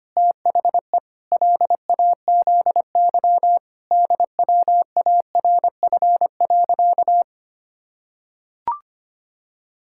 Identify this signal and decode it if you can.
Morse code, heard as a keyed tone, about 25 wpm, 700 Hz. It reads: THE LAZY DWARF.
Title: Pangram Challenge v2 - 25wpm